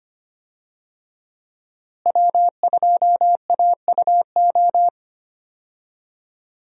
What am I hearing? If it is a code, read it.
Morse code, heard as a keyed tone, about 25 wpm, 700 Hz. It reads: W2AUO
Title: USA Call Signs - 5 Characters - Repeated 3x - 25wpm